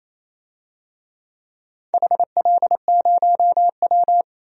Morse code, heard as a keyed tone, about 28 wpm, 700 Hz. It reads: HL0W
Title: DXCC-100 Call Signs - 4 Characters - 28wpm